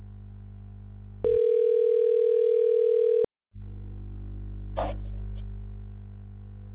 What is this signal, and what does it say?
Ringback tone (North America), 440 + 480 Hz.